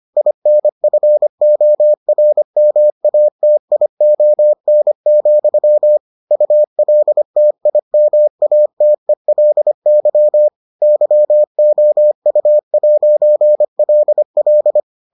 Morse code, heard as a keyed tone, about 25 wpm, 600 Hz. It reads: INFORMATION, ULTIMATELY YOU'LL